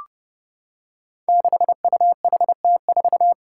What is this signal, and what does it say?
Morse code, heard as a keyed tone, about 31 wpm, 700 Hz. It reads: 6UHT4
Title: Five Challenging Letter-Number Combinations 30wpm